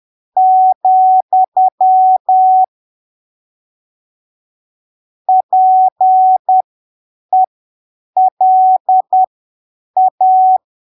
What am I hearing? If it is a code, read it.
Morse code, heard as a keyed tone, about 10 wpm, spaced as for 7 wpm, 750 Hz. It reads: , PELA
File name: arquivos / texto / texto 14 10ppm